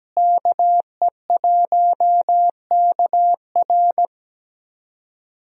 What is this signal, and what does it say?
Morse code, heard as a keyed tone, about 17 wpm, 700 Hz. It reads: KE1KR
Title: USA Call Signs - 5 Characters - Repeated 2x - 17wpm